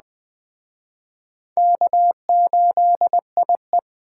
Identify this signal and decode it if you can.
Morse code, heard as a keyed tone, about 20 wpm, 700 Hz. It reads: K8IE
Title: Speed Racing - USA 1x2 Call Signs 17wpm